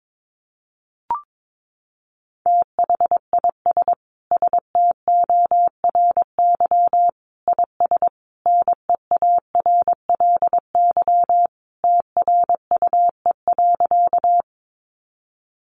Morse code, heard as a keyed tone, about 22 wpm, 700 Hz. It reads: THIS STORY IS NEARLY TRUE.
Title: Sentences from Top 1300 Words 22wpm